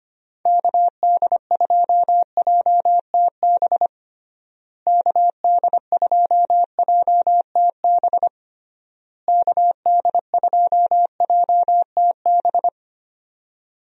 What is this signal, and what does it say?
Morse code, heard as a keyed tone, about 25 wpm, 700 Hz. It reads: KD2JTB KD2JTB KD2JTB
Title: USA Call Signs - 6 Characters - Repeated 3x - 25wpm